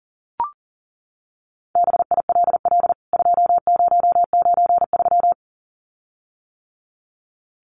Morse code, heard as a keyed tone, about 40 wpm, 700 Hz. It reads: BILL 2093
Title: CWT Exchange 40wpm